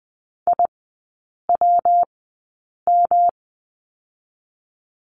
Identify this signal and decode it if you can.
Morse code, heard as a keyed tone, about 20 wpm, spaced as for 8 wpm, 700 Hz. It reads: IWM
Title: 152 Lesson 25e - Letter p - Three Characters 20wpm